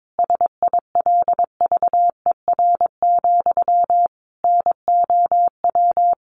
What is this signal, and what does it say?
Morse code, heard as a keyed tone, about 22 wpm, 700 Hz. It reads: SILVER, NOW
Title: A Sound Of Thunder 22wpm